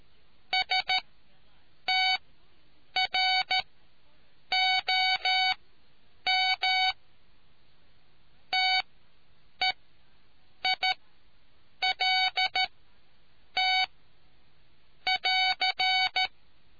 Morse code, heard as a keyed tone, about 13 wpm, spaced as for 7 wpm, 750 Hz. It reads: STROM TEILT <AR>